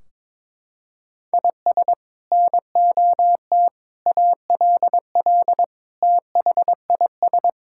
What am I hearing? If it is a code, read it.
Morse code, heard as a keyed tone, about 22 wpm, 700 Hz. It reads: IS NOT ALL THIS
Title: Sentences from Top 1600 Words 22wpm